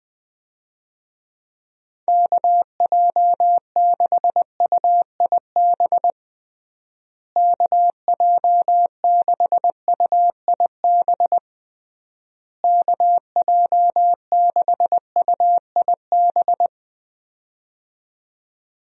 Morse code, heard as a keyed tone, about 21 wpm, 700 Hz. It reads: KJ6UIB KJ6UIB KJ6UIB
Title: USA Call Signs - 6 Characters - Repeated 3x - 20wpm